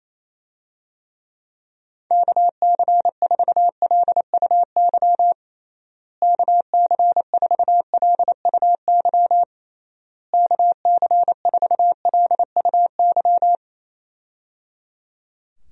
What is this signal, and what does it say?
Morse code, heard as a keyed tone, about 27 wpm, 700 Hz. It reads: KC4LUY KC4LUY KC4LUY
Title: USA Call Signs - 6 Characters - Repeated 3x - 28wpm